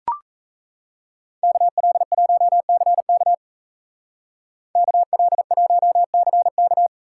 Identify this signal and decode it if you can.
Morse code, heard as a keyed tone, about 34 wpm, 700 Hz. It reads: KL1CK KL1CK
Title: Speed Racing - USA 2x2 Call Signs 28wpm